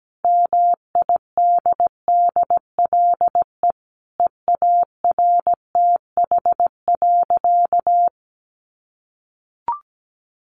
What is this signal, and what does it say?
Morse code, heard as a keyed tone, about 17 wpm, 700 Hz. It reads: MIDDLE EARTH.
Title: Sentences from Top 900 Words 17wpm